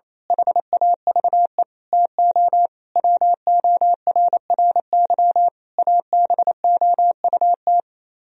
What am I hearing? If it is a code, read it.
Morse code, heard as a keyed tone, about 26 wpm, 700 Hz. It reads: HAVE TO WORRY ABOUT
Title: Speed-Racing - Sets of 5 Words 28wpm